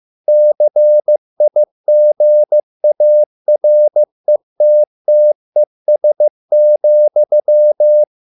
Morse code, heard as a keyed tone, about 15 wpm, 600 Hz. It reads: CIGARETTES,